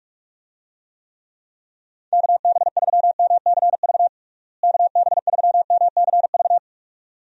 Morse code, heard as a keyed tone, about 46 wpm, 700 Hz. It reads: KB3MCV KB3MCV
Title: USA Call Signs - 6 Characters - Repeated 2x - 45wpm